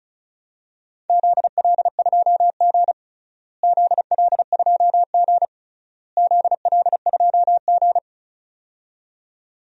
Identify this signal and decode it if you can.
Morse code, heard as a keyed tone, about 34 wpm, 700 Hz. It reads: ZL2G ZL2G ZL2G
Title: DXCC-100 Call Signs - 4 Characters - Repeated 3x - 35wpm